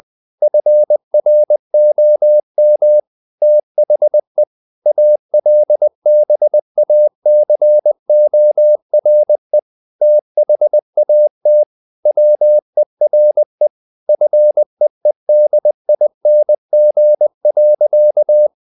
Morse code, heard as a keyed tone, about 20 wpm, 600 Hz. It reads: FROM THE ALBACORE THAT WERE FEEDING.
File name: the old man and the sea / the old man and the sea part 1 20wpm